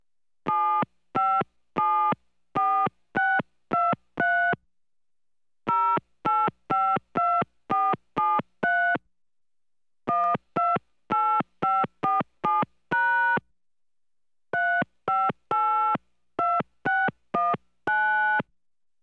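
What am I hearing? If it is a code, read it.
DTMF, heard as telephone touch tones, tones of about 250 ms. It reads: 7574623 0852473 128547# 3582619
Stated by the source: Touch-tone telephone dialing a phone number in an office.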